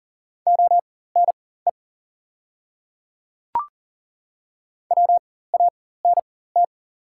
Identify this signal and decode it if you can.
Morse code, heard as a keyed tone, about 40 wpm, spaced as for 19 wpm, 700 Hz. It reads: ONE WANT
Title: 4x Character Spacing - Top 200 Words - 40wpm